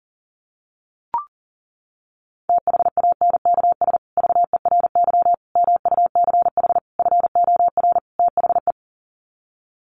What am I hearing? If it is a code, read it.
Morse code, heard as a keyed tone, about 40 wpm, 700 Hz. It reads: THANKS VERY MUCH FOR THE
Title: Sets of 5 Words Encore2 40wpm